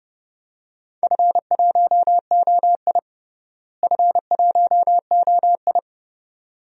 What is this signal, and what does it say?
Morse code, heard as a keyed tone, about 30 wpm, 700 Hz. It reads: F1OI F1OI
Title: DXCC-100 Call Signs - 4 Characters - Repeated 2x - 30wpm